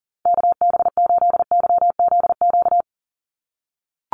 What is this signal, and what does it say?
Morse code, heard as a keyed tone, about 40 wpm, 700 Hz. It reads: KB8YZQ
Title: USA Call Signs 40wpm